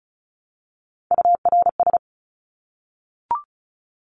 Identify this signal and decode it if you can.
Morse code, heard as a keyed tone, about 34 wpm, 700 Hz. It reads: URS
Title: Abbreviations 35wpm